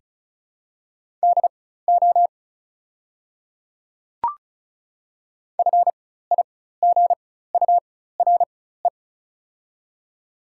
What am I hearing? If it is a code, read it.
Morse code, heard as a keyed tone, about 34 wpm, spaced as for 16 wpm, 690 Hz. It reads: DO FIGURE
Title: 4x Character Spacing - Top 500 Words - 35wpm